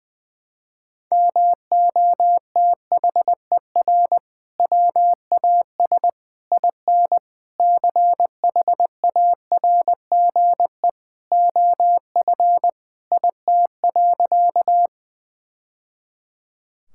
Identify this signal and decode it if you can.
Morse code, heard as a keyed tone, about 20 wpm, 700 Hz. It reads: MOTHER WAS IN CHARGE OF IT.
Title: Sentences from Top 1000 Words - Review Part 6 - 20wpm